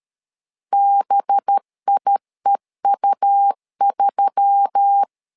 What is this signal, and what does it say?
Morse code, heard as a keyed tone, about 13 wpm, 800 Hz. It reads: BIEU3
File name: LEZIONI / 5 LBVC / 5 LB 12wpm 1906